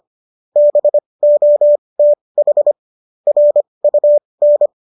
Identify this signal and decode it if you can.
Morse code, heard as a keyed tone, about 25 wpm, spaced as for 20 wpm, 590 Hz. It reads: BOTH RUN